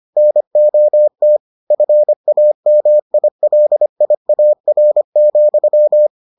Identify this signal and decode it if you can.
Morse code, heard as a keyed tone, about 25 wpm, 600 Hz. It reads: NOT FAMILIAR,